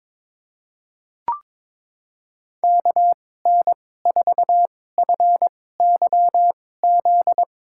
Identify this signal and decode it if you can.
Morse code, heard as a keyed tone, about 22 wpm, spaced as for 16 wpm, 700 Hz. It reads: KN4FYZ
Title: USA Call Signs - 6 Characters - 2x Character Spacing - 22wpm